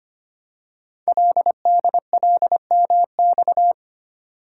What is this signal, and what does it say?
Morse code, heard as a keyed tone, about 25 wpm, 700 Hz. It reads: LDLMX
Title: Code Groups - 5 Letters - 25wpm